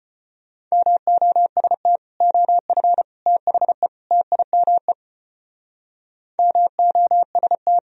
Morse code, heard as a keyed tone, about 31 wpm, 700 Hz. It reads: MOST OF THE TIME MOST